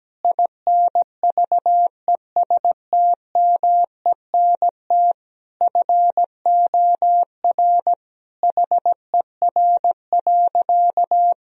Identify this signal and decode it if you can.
Morse code, heard as a keyed tone, about 17 wpm, 700 Hz. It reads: INVESTMENT FOR HER.
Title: Sentences from Top 1000 Words - Review Part 4 - 17wpm